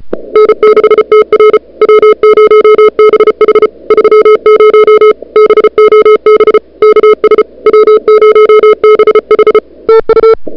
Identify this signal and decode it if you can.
Morse code, heard as a keyed tone, about 35 wpm, 440 Hz. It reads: N6TR W0BH 30 BOB KS W0BH TU
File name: n6tr / Fab2007CWSprintHour1